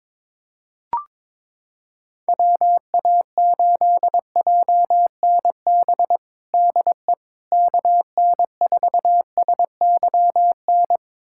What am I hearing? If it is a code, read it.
Morse code, heard as a keyed tone, about 22 wpm, 700 Hz. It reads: WA8JNB DE KN4SYN
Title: USA Callsign DE Callsign - 6 Characters - 22wpm